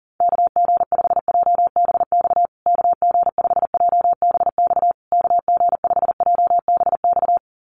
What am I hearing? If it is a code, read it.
Morse code, heard as a keyed tone, about 40 wpm, 700 Hz. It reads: KG5JBX KG5JBX KG5JBX
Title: USA Call Signs Repeated 3x 40wpm